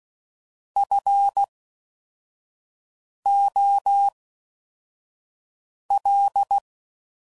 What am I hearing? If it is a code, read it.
Morse code, heard as a keyed tone, about 16 wpm, spaced as for 4 wpm, 800 Hz. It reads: FOL